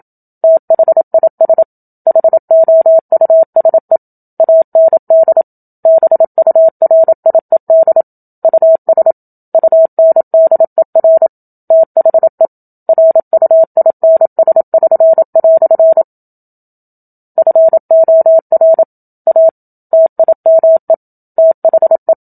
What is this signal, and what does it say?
Morse code, heard as a keyed tone, about 27 wpm, 650 Hz. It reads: THIS HOUSE AND BURIED US UNDER THE RUINS<SN>" FOR A TIME THE
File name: Book Two / 01-Chapter 1 - 27 WPM